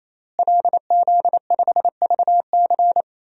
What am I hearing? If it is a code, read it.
Morse code, heard as a keyed tone, about 28 wpm, 700 Hz. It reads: LZ5VC
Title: DXCC-100 Call Signs - 5 Characters - Repeated 3x - 28wpm